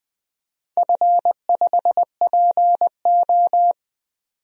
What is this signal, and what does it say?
Morse code, heard as a keyed tone, about 20 wpm, 700 Hz. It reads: F5PO
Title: DXCC-100 Call Signs - 4 Characters - 20wpm